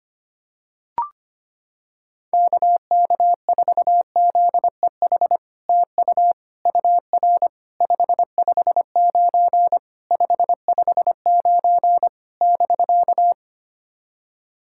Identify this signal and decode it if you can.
Morse code, heard as a keyed tone, about 25 wpm, 700 Hz. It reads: KK4ZEH TU UR 559 559 <BK>
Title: SOTA QSO Practice 25wpm